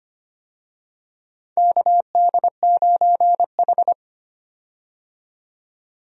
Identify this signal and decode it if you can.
Morse code, heard as a keyed tone, about 25 wpm, 700 Hz. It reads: KD9H
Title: USA Call Signs - 4 Characters - 25wpm